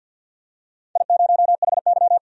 Morse code, heard as a keyed tone, about 52 wpm, 700 Hz. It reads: I0HY
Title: DXCC-100 Call Signs - 4 Characters - 50wpm